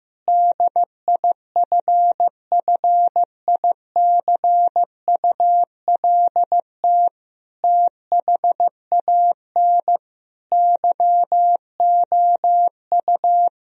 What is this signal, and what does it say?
Morse code, heard as a keyed tone, about 16 wpm, 700 Hz. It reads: DIFFICULT THAN YOU
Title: Sentences from Top 1000 Words - Review Part 3 - 15wpm